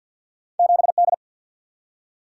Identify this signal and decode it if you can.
Morse code, heard as a keyed tone, about 47 wpm, 700 Hz. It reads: 7D
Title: Single-Letter Single-Number 50wpm